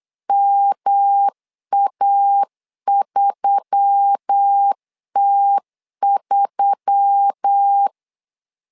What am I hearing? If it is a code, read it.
Morse code, heard as a keyed tone, about 8 wpm, 800 Hz. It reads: MA3T3
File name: LEZIONI / 4 Q1S3 8wpm 3037